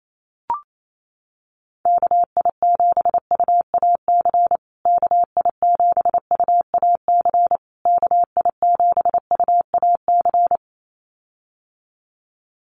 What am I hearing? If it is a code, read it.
Morse code, heard as a keyed tone, about 28 wpm, 700 Hz. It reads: KI7UAC KI7UAC KI7UAC
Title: USA Call Signs Repeated 3x 28wpm